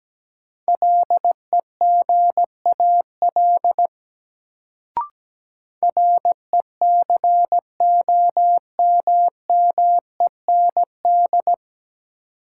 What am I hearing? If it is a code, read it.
Morse code, heard as a keyed tone, about 17 wpm, 700 Hz. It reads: LEGAL RECOMMEND
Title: Sentences from Top 1200 Words - New Word Drill - 17wpm